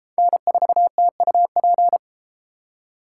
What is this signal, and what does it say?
Morse code, heard as a keyed tone, about 30 wpm, 700 Hz. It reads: N4TUP